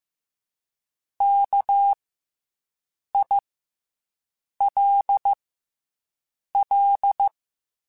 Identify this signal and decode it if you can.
Morse code, heard as a keyed tone, about 15 wpm, 800 Hz. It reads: K I L L
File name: Special INGs / Special INGs-mp3s / SlowToFaster ing @13